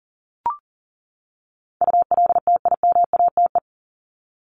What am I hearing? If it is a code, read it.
Morse code, heard as a keyed tone, about 40 wpm, 700 Hz. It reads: ULTIMATE
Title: LY Suffix - Top 100 Words with Base Words 40wpm